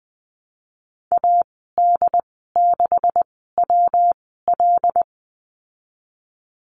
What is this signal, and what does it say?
Morse code, heard as a keyed tone, about 19 wpm, spaced as for 14 wpm, 700 Hz. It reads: AD6WL
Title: 193 Lesson 31g - Letter v - Call Signs 20wpm